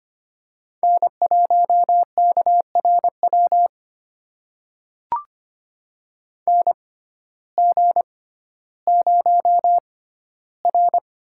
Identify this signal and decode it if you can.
Morse code, heard as a keyed tone, about 25 wpm, spaced as for 14 wpm, 700 Hz. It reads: N1KRW NG0R
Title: USA Call Signs - 6x Character Spacing - 25wpm